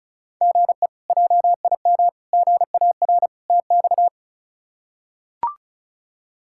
Morse code, heard as a keyed tone, about 34 wpm, 700 Hz. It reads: GE JIM GAR TX
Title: K1USN SST QSO Practice - 35wpm